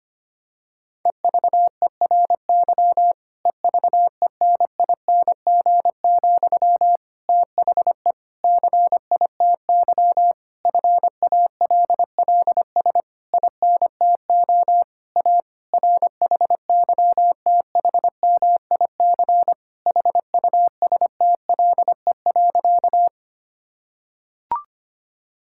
Morse code, heard as a keyed tone, about 25 wpm, 700 Hz. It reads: EVERY EVENING, THE CITY FALLS INTO A RHYTHMIC HUSTLE.
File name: practice / 25wpm / Difficult Sentences with Hints - 25wpm